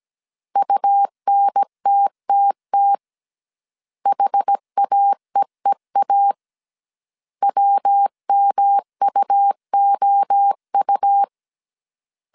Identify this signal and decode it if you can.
Morse code, heard as a keyed tone, about 17 wpm, 800 Hz. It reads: UNTTT HAEEA WMUOU